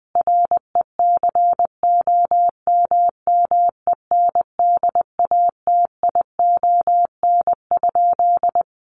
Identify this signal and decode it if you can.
Morse code, heard as a keyed tone, about 20 wpm, 700 Hz. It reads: RECOMMENDATION?
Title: Sentences from Top 1400 Words 20wpm